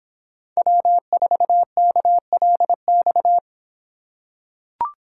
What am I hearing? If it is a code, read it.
Morse code, heard as a keyed tone, about 26 wpm, 700 Hz. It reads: W4KLX